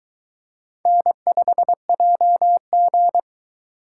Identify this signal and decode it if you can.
Morse code, heard as a keyed tone, about 22 wpm, 700 Hz. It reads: N5JG